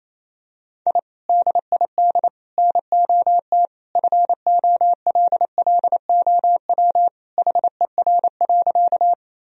Morse code, heard as a keyed tone, about 27 wpm, 700 Hz. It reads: I DID NOT FOLLOW HER.
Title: Sentences from Top 1000 Words - Review Part 1 - 28wpm